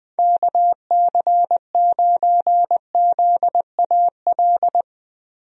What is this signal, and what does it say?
Morse code, heard as a keyed tone, about 21 wpm, 700 Hz. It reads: KC9ZAL
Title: USA Call Signs - 6 Characters - Repeated 2x - 20wpm